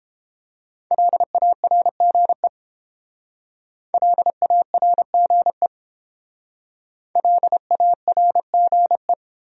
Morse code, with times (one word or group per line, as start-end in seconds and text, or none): 0.9-2.5 LARGE
3.9-5.7 LARGE
7.2-9.1 LARGE